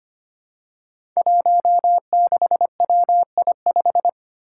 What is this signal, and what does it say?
Morse code, heard as a keyed tone, about 25 wpm, 700 Hz. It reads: 16WI5
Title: Five Challenging Letter-Number Combinations 25wpm